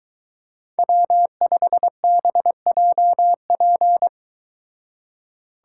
Morse code, heard as a keyed tone, about 22 wpm, 700 Hz. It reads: W5BJP